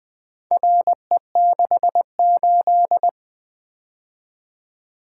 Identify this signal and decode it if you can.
Morse code, heard as a keyed tone, about 20 wpm, 700 Hz. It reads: RE68
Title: Four Letter-Number Combinations - 20wpm